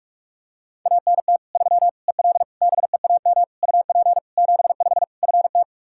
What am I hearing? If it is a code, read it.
Morse code, heard as a keyed tone, about 46 wpm, 700 Hz. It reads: ANT 3 EL BEAM UP 75 FT
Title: Practice QSO - Exchange 2 - Info - 45wpm